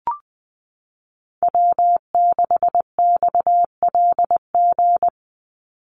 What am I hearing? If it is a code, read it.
Morse code, heard as a keyed tone, about 20 wpm, 700 Hz. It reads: W6XLG
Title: USA Call Signs 20wpm